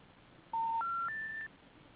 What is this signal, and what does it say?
Special information tone.